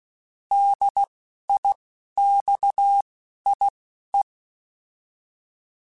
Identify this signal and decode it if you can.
Morse code, heard as a keyed tone, about 16 wpm, spaced as for 12 wpm, 800 Hz. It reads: DIXIE
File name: Alphabetical / Alphabetical mp3s / Ds @10